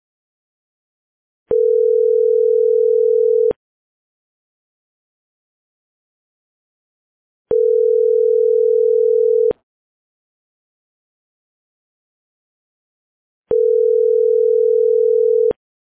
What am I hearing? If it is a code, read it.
Ringback tone (North America), 440 + 480 Hz.